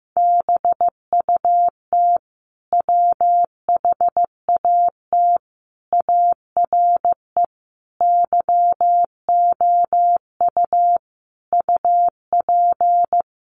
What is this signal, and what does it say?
Morse code, heard as a keyed tone, about 15 wpm, 700 Hz. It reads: BUT WHAT ARE YOU UP
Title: Sentences from Top 100 Words 15wpm